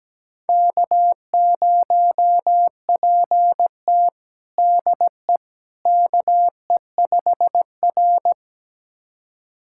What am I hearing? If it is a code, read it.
Morse code, heard as a keyed tone, about 17 wpm, 700 Hz. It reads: K0PT DE KE5R
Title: USA Callsign DE Callsign - 4 Characters - 17wpm